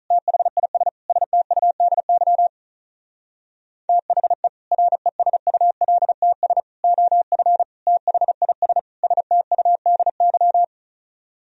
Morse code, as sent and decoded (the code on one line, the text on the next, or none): - .... .. ... / ... - ..- -.. -.-- / - .... / .-. ... ..- .-.. - ... / --- ..-. / - .... .. ... / ... - ..- -.. -.--
THIS STUDY THE RESULTS OF THIS STUDY